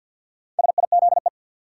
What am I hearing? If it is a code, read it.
Morse code, heard as a keyed tone, about 52 wpm, 700 Hz. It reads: SIZE
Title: Sets of 3 Words - New Word Drill - 50wpm